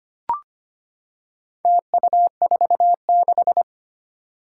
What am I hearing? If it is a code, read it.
Morse code, heard as a keyed tone, about 25 wpm, 700 Hz. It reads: TU46
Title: Four Challenging Letter-Number Combinations 25wpm